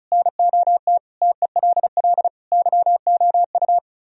Morse code, heard as a keyed tone, about 43 wpm, 700 Hz. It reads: NOT TELL YOU
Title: Speed-Racing - Sets of 5 Words 35wpm